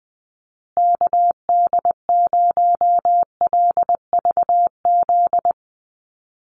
Morse code, heard as a keyed tone, about 20 wpm, 700 Hz. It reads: KD0LVZ
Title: USA Call Signs Repeated 3x 20wpm